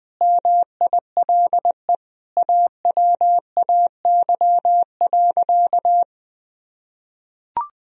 Morse code, heard as a keyed tone, about 21 wpm, 700 Hz. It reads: MILE AWAY.
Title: Sentences from Top 1000 Words - Review Part 1 - 20wpm